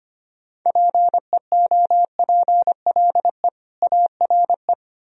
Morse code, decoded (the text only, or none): PEOPLE ARE